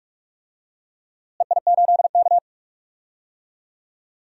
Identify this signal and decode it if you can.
Morse code, heard as a keyed tone, about 44 wpm, 690 Hz. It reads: EI8K
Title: DXCC-100 Call Signs - 4 Characters - Repeated 3x - 45wpm